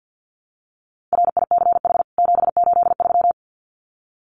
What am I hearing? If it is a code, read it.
Morse code, heard as a keyed tone, about 48 wpm, 700 Hz. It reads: RICH 783
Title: CWT Exchange 50wpm